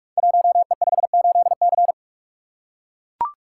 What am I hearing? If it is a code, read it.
Morse code, heard as a keyed tone, about 44 wpm, 690 Hz. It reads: 1E58C
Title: Code Groups - 5 Letter:Numbers - 45wpm